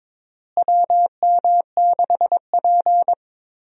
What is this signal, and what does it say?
Morse code, heard as a keyed tone, about 21 wpm, 700 Hz. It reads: WM6P
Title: USA Call Signs - 4 Characters - Repeated 3x - 22wpm